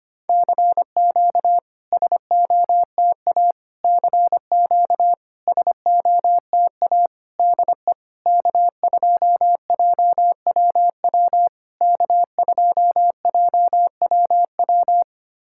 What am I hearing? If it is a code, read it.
Morse code, heard as a keyed tone, about 25 wpm, 690 Hz. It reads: CQ SOTA CQ SOTA DE K2JWW K2JWW
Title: SOTA QSO Practice 25wpm